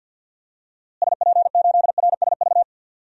Morse code, heard as a keyed tone, about 47 wpm, 700 Hz. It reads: SP8RSV
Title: DXCC-100 Call Signs - 6 Characters - Repeated 2x - 50wpm